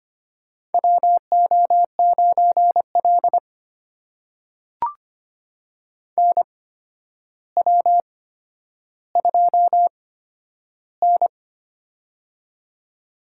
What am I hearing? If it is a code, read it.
Morse code, heard as a keyed tone, about 25 wpm, spaced as for 11 wpm, 700 Hz. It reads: WO9L NW2N
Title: USA Call Signs - 4 Characters - 8x Character Spacing - 25wpm